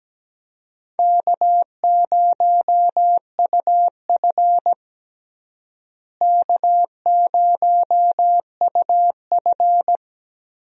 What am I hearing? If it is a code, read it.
Morse code, heard as a keyed tone, about 17 wpm, 700 Hz. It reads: K0UF K0UF
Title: USA Call Signs - 4 Characters - Repeated 3x - 17wpm